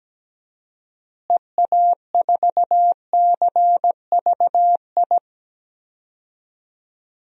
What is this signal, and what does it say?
Morse code, heard as a keyed tone, about 17 wpm, 700 Hz. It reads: EA4CVI
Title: DXCC-100 Call Signs - 6 Characters - Repeated 2x - 17wpm